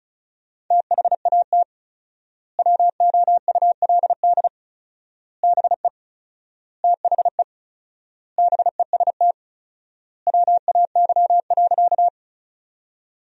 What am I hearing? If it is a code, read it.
Morse code, heard as a keyed tone, about 36 wpm, 700 Hz. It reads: THAT WOULD BE THE BEST WAY.
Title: 4x Word-Spacing - Sentences from Top 400 Words - 35wpm